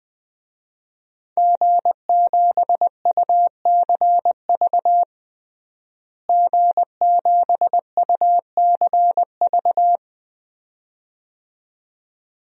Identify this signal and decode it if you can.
Morse code, heard as a keyed tone, about 20 wpm, 700 Hz. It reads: G7UCV G7UCV
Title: DXCC-100 Call Signs - 5 Characters - Repeated 2x - 20wpm